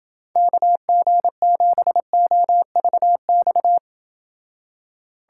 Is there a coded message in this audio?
Yes: Morse code from 0.4 to 3.8 s, heard as a keyed tone, about 28 wpm, 700 Hz. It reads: KG7OVX